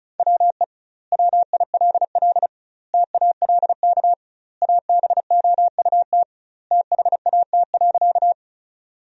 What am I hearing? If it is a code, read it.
Morse code, heard as a keyed tone, about 36 wpm, 700 Hz. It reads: WE WILL TALK ABOUT THAT.
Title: 2x Word-Spacing - Sentences from Top 400 Words - 35wpm